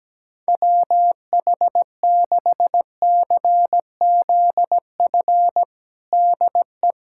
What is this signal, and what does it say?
Morse code, heard as a keyed tone, about 17 wpm, 700 Hz. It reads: WH6CZF DE
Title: USA Callsign DE Callsign - 6 Characters - 17wpm